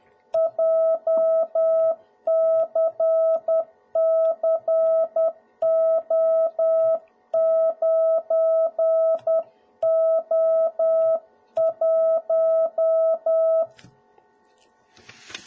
Morse code, heard as a keyed tone, about 10 wpm, 650 Hz. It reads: JCCO9O1
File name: jh8cbh / 2021cw / cw01b